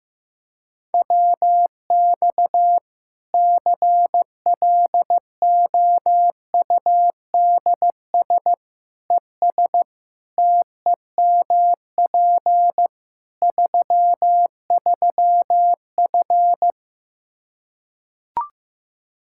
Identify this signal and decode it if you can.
Morse code, heard as a keyed tone, about 15 wpm, 700 Hz. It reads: WX CLOUDS ES TEMP 33F
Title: K1USN SST QSO Practice 15wpm